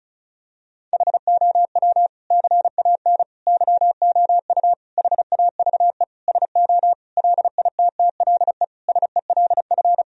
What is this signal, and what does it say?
Morse code, heard as a keyed tone, about 34 wpm, 700 Hz. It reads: HOW CAN YOU HAVE SO LITTLE SELF
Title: Sentences from Top 1000 Words - Review Part 1 - 35wpm